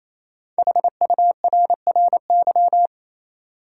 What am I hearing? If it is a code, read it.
Morse code, heard as a keyed tone, about 28 wpm, 700 Hz. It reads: HURRY
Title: Five Letter Words - 28wpm